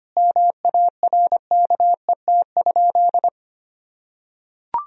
Morse code, heard as a keyed tone, about 25 wpm, 700 Hz. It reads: MARKET?
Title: Sentences from Top 1000 Words - Review Part 3 - 25wpm